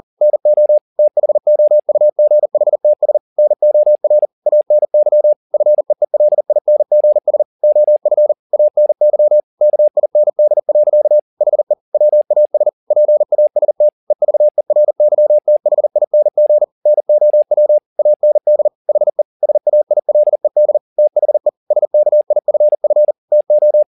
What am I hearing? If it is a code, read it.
Morse code, heard as a keyed tone, about 40 wpm, 600 Hz. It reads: NO THOUGHTS NOR ANY FEELINGS OF ANY KIND. HE WAS PAST EVERYTHING NOW AND HE SAILED THE SKIFF TO
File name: the old man and the sea 2 40wpm